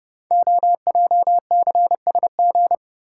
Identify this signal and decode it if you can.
Morse code, heard as a keyed tone, about 30 wpm, 690 Hz. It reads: OJCSG
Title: Code Groups - 5 Letters - 30wpm